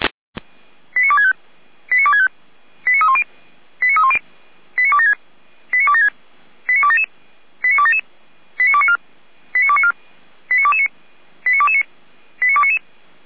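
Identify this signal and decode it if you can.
Five-tone selective call (ZVEI), heard as quick runs of five tones, tones of about 70 ms. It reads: 89265 89265 89310 89310 89276 89276 89277 89277 89294 89294 89229 89229 89299
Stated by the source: ZVEI five-tone call 89173 received via SDR, from the feuerwehr-orlamuende (Orlamünde fire brigade) audio server.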